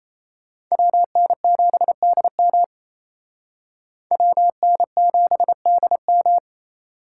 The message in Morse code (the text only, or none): WN7DM WN7DM